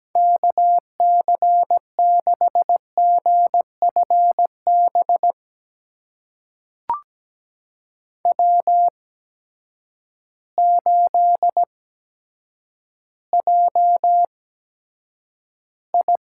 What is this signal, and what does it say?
Morse code, heard as a keyed tone, about 17 wpm, spaced as for 8 wpm, 700 Hz. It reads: KC6GFB W8JI
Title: USA Call Signs - 8x Character Spacing - 17wpm